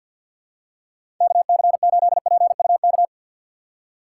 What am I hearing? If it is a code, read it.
Morse code, heard as a keyed tone, about 52 wpm, 690 Hz. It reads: KC8PUK
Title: USA Call Signs - 6 Characters - 3x Character Spacing - 50wpm